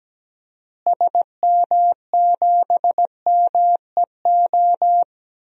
Morse code, heard as a keyed tone, about 17 wpm, 700 Hz. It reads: SM7MEO
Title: DXCC-100 Call Signs - 6 Characters - Repeated 3x - 17wpm